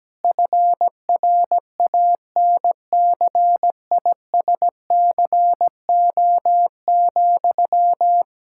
Morse code, heard as a keyed tone, about 17 wpm, 700 Hz. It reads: FRANCISCO,
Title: K1USN SST QSO Practice 17wpm